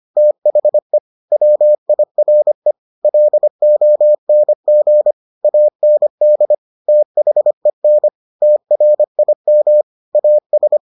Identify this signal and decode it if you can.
Morse code, heard as a keyed tone, about 25 wpm, 600 Hz. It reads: THE WIRE LONG AND THEN TRIM AS